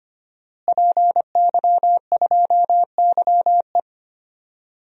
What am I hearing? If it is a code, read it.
Morse code, heard as a keyed tone, about 25 wpm, 700 Hz. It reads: PY2YE
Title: DXCC-100 Call Signs - 5 Characters - Repeated 2x - 25wpm